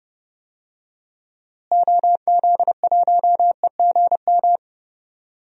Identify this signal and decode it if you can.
Morse code, heard as a keyed tone, about 32 wpm, 700 Hz. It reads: OZ1EGM
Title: DXCC-100 Call Signs - 6 Characters - Repeated 3x - 30wpm